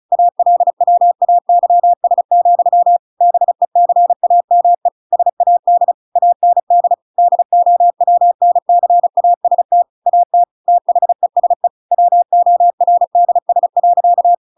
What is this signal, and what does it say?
Morse code, heard as a keyed tone, about 34 wpm, 700 Hz. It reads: ALWAYS, BECAME SAD AND DOWNCAST AT THESE WORDS.
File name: Pinocchio / Pinocchio ch8 35 35 ik1hge